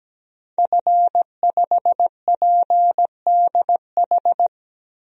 Morse code, heard as a keyed tone, about 17 wpm, 700 Hz. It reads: F5PDH
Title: DXCC-100 Call Signs - 5 Characters - 17wpm